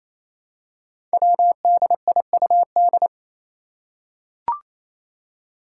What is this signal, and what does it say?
Morse code, heard as a keyed tone, about 28 wpm, 700 Hz. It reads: WDIUD
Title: Code Groups - 5 Letter:Numbers - 28wpm